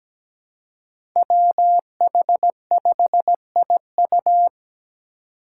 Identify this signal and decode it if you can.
Morse code, heard as a keyed tone, about 17 wpm, 700 Hz. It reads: WH5IU
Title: Five Challenging Letter-Number Combinations 17wpm